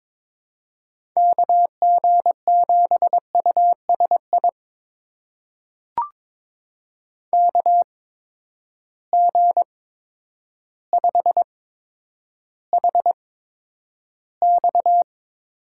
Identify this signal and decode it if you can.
Morse code, heard as a keyed tone, about 22 wpm, spaced as for 10 wpm, 700 Hz. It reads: KG7USI KG5HX
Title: USA Call Signs - 8x Character Spacing - 22wpm